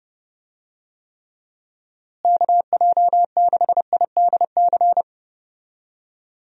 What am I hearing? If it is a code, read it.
Morse code, heard as a keyed tone, about 31 wpm, 700 Hz. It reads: KJ6IDC
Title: USA Call Signs - 6 Characters - Repeated 2x - 30wpm